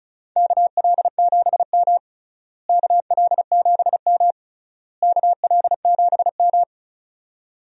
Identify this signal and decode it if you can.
Morse code, heard as a keyed tone, about 36 wpm, 700 Hz. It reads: KL7M KL7M KL7M
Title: DXCC-100 Call Signs - 4 Characters - Repeated 3x - 35wpm